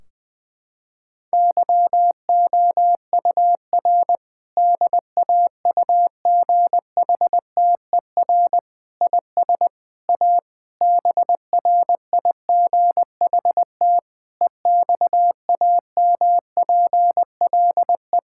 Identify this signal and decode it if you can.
Morse code, heard as a keyed tone, about 20 wpm, 700 Hz. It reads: YOUR DAUGHTER IS A BRIGHT EXAMPLE
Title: Sentences from Top 1600 Words 20wpm